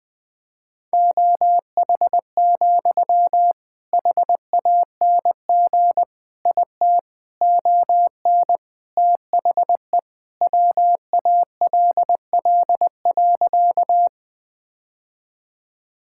Morse code, heard as a keyed tone, about 20 wpm, 700 Hz. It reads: OH, HANG IT ON THE WALL.
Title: Sentences from Top 1600 Words 20wpm